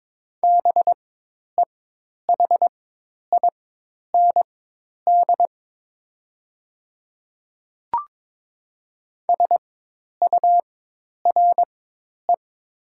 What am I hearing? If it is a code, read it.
Morse code, heard as a keyed tone, about 22 wpm, spaced as for 10 wpm, 700 Hz. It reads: BEHIND SURE
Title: 4x Character Spacing - Top 500 Words - 22wpm